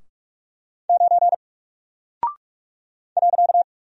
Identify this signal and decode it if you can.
Morse code, heard as a keyed tone, about 44 wpm, 700 Hz. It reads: 9 .
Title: Single Character (Letter:Number:Punctuation:Prosign) 45wpm